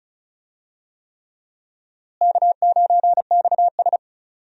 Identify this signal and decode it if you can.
Morse code, heard as a keyed tone, about 36 wpm, 700 Hz. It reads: K9XS
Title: USA Call Signs - 4 Characters - Repeated 3x - 35wpm